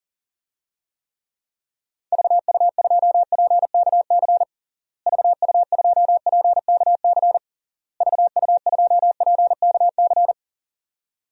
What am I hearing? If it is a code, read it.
Morse code, heard as a keyed tone, about 40 wpm, 700 Hz. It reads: VU2PKC VU2PKC VU2PKC
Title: DXCC-100 Call Signs - 6 Characters - Repeated 3x - 40wpm